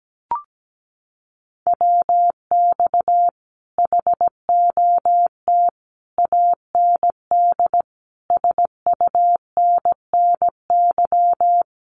Morse code, heard as a keyed tone, about 17 wpm, 700 Hz. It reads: WX HOT AND SUNNY
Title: Common QSO Phrases Encore 17wpm